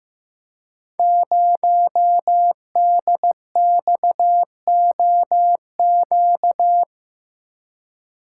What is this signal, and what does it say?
Morse code, heard as a keyed tone, about 15 wpm, 700 Hz. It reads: 0DXOQ
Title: Code Groups - 5 Letter:Numbers - 15wpm